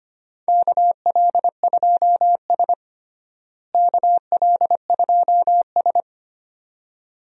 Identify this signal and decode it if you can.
Morse code, heard as a keyed tone, about 25 wpm, 700 Hz. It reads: KL2S KL2S
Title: USA Call Signs - 4 Characters - Repeated 2x - 25wpm